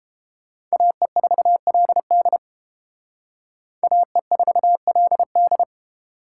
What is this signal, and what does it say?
Morse code, heard as a keyed tone, about 30 wpm, 700 Hz. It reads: AE4LD AE4LD